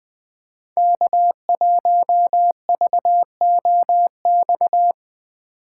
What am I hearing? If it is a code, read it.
Morse code, heard as a keyed tone, about 24 wpm, 700 Hz. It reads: K1VOX
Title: Speed Racing - USA 1x3 Call Signs 20wpm